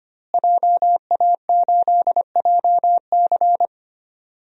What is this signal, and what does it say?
Morse code, heard as a keyed tone, about 25 wpm, 700 Hz. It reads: JA8JC
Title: DXCC-100 Call Signs - 5 Characters - Repeated 2x - 25wpm